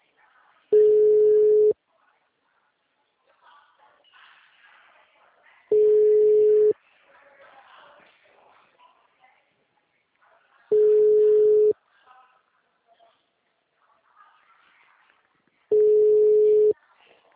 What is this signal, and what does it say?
Ringback tone (Europe, 425 Hz), with speech.